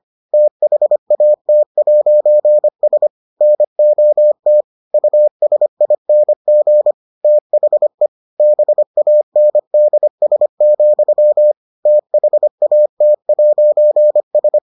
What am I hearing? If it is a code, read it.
Morse code, heard as a keyed tone, about 25 wpm, 600 Hz. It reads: THAT'S NOT USING THE BANDS, THAT'S